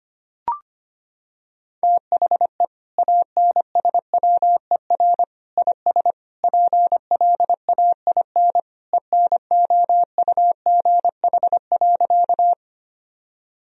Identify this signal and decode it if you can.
Morse code, heard as a keyed tone, about 25 wpm, 700 Hz. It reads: THE ANSWER IS PLAIN ENOUGH.
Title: Sentences from Top 1000 Words - Review Part 2 - 25wpm